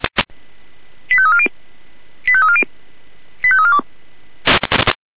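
Five-tone selective call (ZVEI), heard as quick runs of five tones, tones of about 70 ms. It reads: 85340 85340 85342
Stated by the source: ZVEI five-tone selective call received by SDR.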